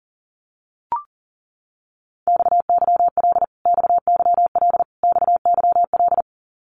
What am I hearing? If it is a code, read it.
Morse code, heard as a keyed tone, about 40 wpm, 700 Hz. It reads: XYL XYL XYL
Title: Abbreviations Repeated 3x - 40wpm